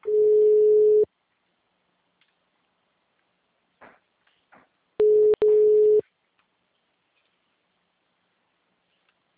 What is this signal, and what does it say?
Ringback tone (Europe, 425 Hz).
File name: out- -2035- - - .68032